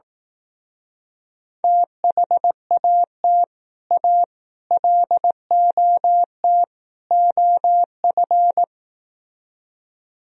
Morse code, heard as a keyed tone, about 18 wpm, 700 Hz. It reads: THAT A LOT OF
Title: Speed-Racing - Sets of 4 Words 15wpm